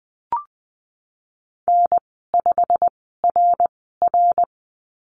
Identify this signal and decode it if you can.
Morse code, heard as a keyed tone, about 19 wpm, spaced as for 14 wpm, 700 Hz. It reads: N5RR
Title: 161 Lesson 26g - Letter g - Call Signs 20wpm